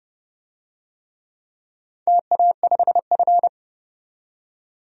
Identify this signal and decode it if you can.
Morse code, heard as a keyed tone, about 30 wpm, 700 Hz. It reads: TA5F
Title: DXCC-100 Call Signs - 4 Characters - 30wpm